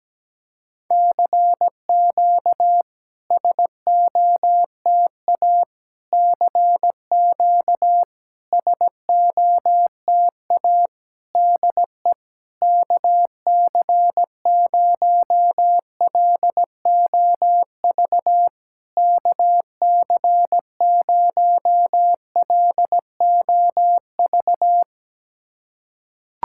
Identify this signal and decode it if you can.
Morse code, heard as a keyed tone, about 17 wpm, 700 Hz. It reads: CQ SOTA CQ SOTA DE KC0LOV KC0LOV
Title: SOTA QSO Practice 17wpm